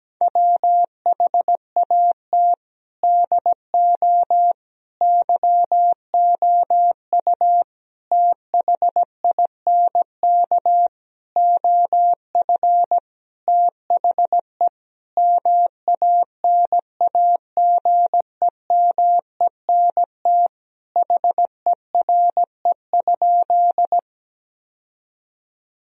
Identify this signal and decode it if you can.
Morse code, heard as a keyed tone, about 17 wpm, 700 Hz. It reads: WHAT DO YOU THINK OF THE MANAGEMENT HERE?
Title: Sentences from Top 1000 Words - Review Part 4 - 17wpm